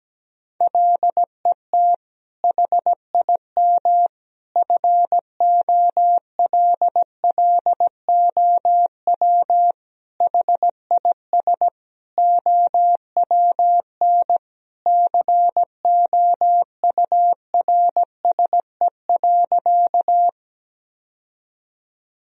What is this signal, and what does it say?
Morse code, heard as a keyed tone, about 17 wpm, 700 Hz. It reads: LET HIM FOLLOW HIS OWN COURSE.
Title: Sentences from Top 1000 Words - Review Part 1 - 17wpm